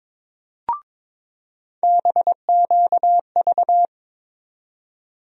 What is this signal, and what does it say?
Morse code, heard as a keyed tone, about 22 wpm, 700 Hz. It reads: BQV
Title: USA Call Sign Postfixes - 3 Letters 22wpm